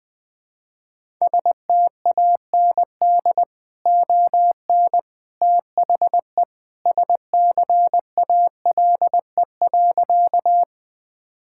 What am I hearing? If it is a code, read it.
Morse code, heard as a keyed tone, about 21 wpm, 700 Hz. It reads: STAND ON THE SCALE.
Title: Sentences from Top 1000 Words - Review Part 5 - 20wpm